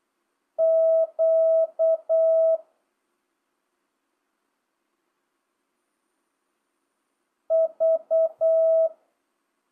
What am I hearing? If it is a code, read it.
Morse code, heard as a keyed tone, about 8 wpm, 650 Hz. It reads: Q V
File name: jh8cbh / 2025cw